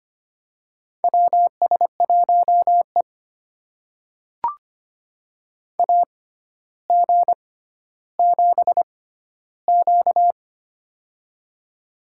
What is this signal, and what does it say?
Morse code, heard as a keyed tone, about 25 wpm, spaced as for 13 wpm, 700 Hz. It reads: WS1E AG7Q
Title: USA Call Signs - 4 Characters - 6x Character Spacing - 25wpm